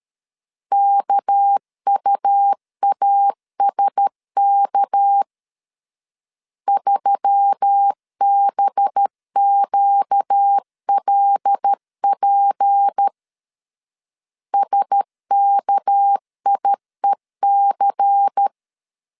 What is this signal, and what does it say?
Morse code, heard as a keyed tone, about 13 wpm, 800 Hz. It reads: KUASK 3BQLP SKIEC
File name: LEZIONI / 6 PKX0 / 6 PK 12wpm 2006